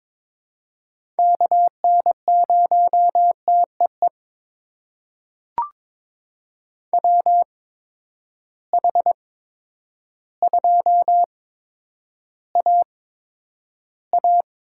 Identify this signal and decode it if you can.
Morse code, heard as a keyed tone, about 22 wpm, spaced as for 10 wpm, 700 Hz. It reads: KN0TEE WH2AA
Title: USA Call Signs - 8x Character Spacing - 22wpm